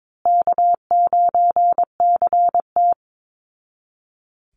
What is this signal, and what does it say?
Morse code, heard as a keyed tone, about 22 wpm, 700 Hz. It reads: K9CT
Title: Call Signs Repeated 3x - 22wpm